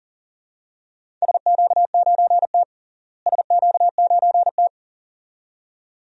Morse code, heard as a keyed tone, about 40 wpm, 700 Hz. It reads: SQ9T SQ9T
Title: DXCC-100 Call Signs - 4 Characters - Repeated 2x - 40wpm